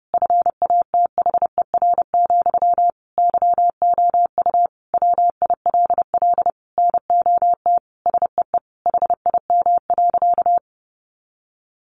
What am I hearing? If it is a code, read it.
Morse code, heard as a keyed tone, about 30 wpm, 700 Hz. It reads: FATHER, YOU WILL NOT SEE HIM.
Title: Sentences from Top 600 Words 30wpm